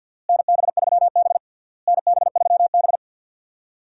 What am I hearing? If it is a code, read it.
Morse code, heard as a keyed tone, about 48 wpm, 700 Hz. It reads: NB3B NB3B
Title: USA Call Signs - 4 Characters - Repeated 3x - 50wpm